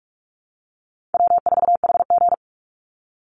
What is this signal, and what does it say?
Morse code, heard as a keyed tone, about 44 wpm, 700 Hz. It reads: W4HG
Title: Call Signs Repeated 2x - 45wpm